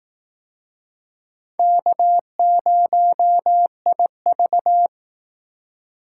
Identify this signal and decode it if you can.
Morse code, heard as a keyed tone, about 19 wpm, 700 Hz. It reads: K0IV
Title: Speed Racing - USA 1x2 Call Signs 15wpm